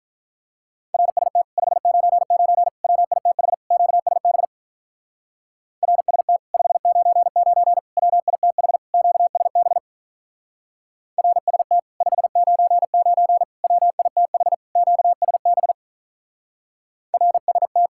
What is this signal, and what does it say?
Morse code, heard as a keyed tone, about 44 wpm, 700 Hz. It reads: RST 599 WITH QSB RST 599 WITH QSB RST 599 WITH QSB RST